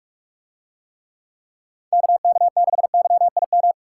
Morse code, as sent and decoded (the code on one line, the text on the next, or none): -.- -.- -.... -.-- .. --
KK6YIM